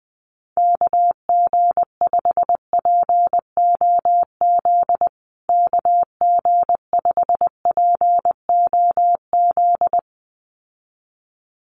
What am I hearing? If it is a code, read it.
Morse code, heard as a keyed tone, about 19 wpm, 700 Hz. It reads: KG5POZ KG5POZ
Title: USA Call Signs Repeated 2x 20wpm